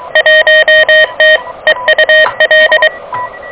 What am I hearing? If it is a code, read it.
Morse code, heard as a keyed tone, about 23 wpm, 1850 Hz, with noise. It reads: 1T EUL
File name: OK QRP 2023 / OK QRP 2023